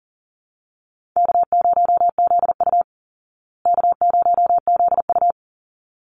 Morse code, heard as a keyed tone, about 40 wpm, 700 Hz. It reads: K0ZU K0ZU
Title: Easy Call Signs Repeated 2x - 40wpm